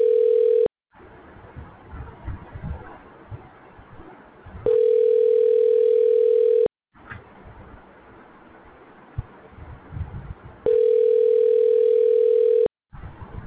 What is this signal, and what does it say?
Ringback tone (North America), 440 + 480 Hz.